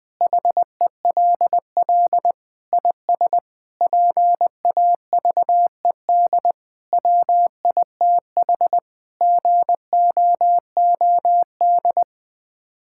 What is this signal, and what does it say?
Morse code, heard as a keyed tone, about 20 wpm, 700 Hz. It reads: HELL IS PAVED WITH GOOD
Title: Sets of 8 Words - 20wpm